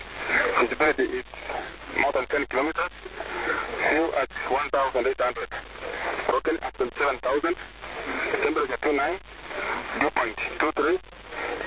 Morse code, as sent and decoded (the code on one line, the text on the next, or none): none